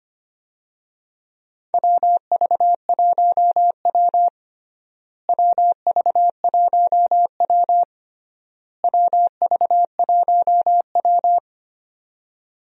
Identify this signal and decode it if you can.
Morse code, heard as a keyed tone, about 25 wpm, 700 Hz. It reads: WV1W WV1W WV1W
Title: USA Call Signs - 4 Characters - Repeated 3x - 25wpm